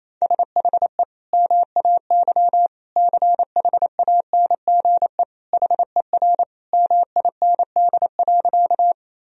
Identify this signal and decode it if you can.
Morse code, heard as a keyed tone, about 28 wpm, 700 Hz. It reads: SHE MAY CHANGE HER MIND.
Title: Sentences from Top 1000 Words - Review Part 1 - 28wpm